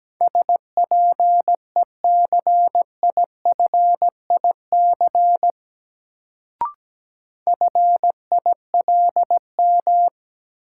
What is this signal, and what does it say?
Morse code, heard as a keyed tone, about 17 wpm, 700 Hz. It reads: SPECIFIC FILM
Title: Sentences from Top 1000 Words - Review Part 3 - New Word Drill - 17wpm